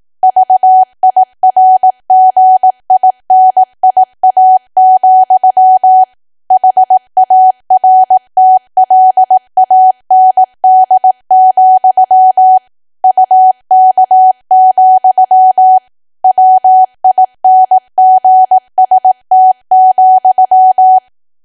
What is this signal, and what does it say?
Morse code, heard as a keyed tone, about 18 wpm, 750 Hz. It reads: VIRGINIA, HARTLAND, UK, WINGST,
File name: cwa / W1AW-Level2 / 18WPM